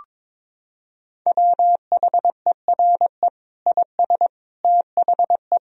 Morse code, heard as a keyed tone, about 22 wpm, 700 Hz. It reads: WHERE IS THE
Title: Sentences from Top 1000 Words - Review Part 2 - 22wpm